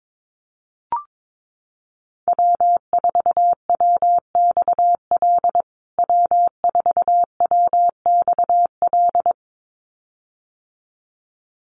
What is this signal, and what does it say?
Morse code, heard as a keyed tone, about 22 wpm, 700 Hz. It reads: W4WXL W4WXL
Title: USA Call Signs Repeated 2x 22wpm